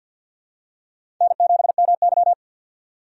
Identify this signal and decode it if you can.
Morse code, heard as a keyed tone, about 50 wpm, 700 Hz. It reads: N7MY
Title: USA Call Signs - 4 Characters - 50wpm